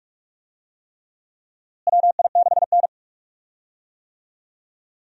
Morse code, heard as a keyed tone, about 41 wpm, 700 Hz. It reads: WI6N
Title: USA Call Signs - 4 Characters - Repeated 3x - 45wpm